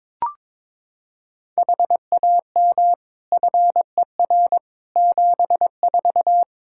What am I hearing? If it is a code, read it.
Morse code, heard as a keyed tone, about 22 wpm, 700 Hz. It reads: HAM FER 74
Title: Practice QSO - Exchange 3 - More Info - 22wpm